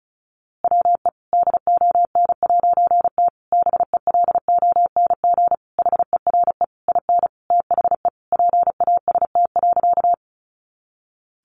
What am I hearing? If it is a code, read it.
Morse code, heard as a keyed tone, about 34 wpm, 700 Hz. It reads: WE DON'T BELONG HERE IN THE PAST.
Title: A Sound Of Thunder 35wpm